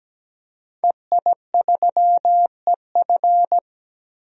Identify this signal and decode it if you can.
Morse code, heard as a keyed tone, about 17 wpm, 700 Hz. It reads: EI3EF
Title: DXCC-100 Call Signs - 5 Characters - 17wpm